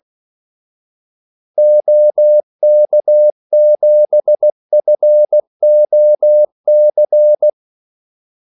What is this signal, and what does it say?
Morse code, heard as a keyed tone, about 16 wpm, 600 Hz. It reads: OK7FOC
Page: Kaldesignaler 16 wpm